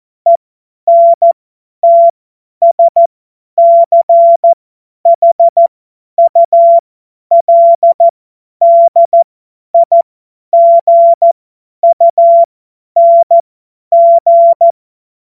Morse code, heard as a keyed tone, about 14 wpm, spaced as for 10 wpm, 680 Hz. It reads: ENTSCHULDIGUNG
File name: Morsekurs / Witze / Witz 30. November 2021, 50 F70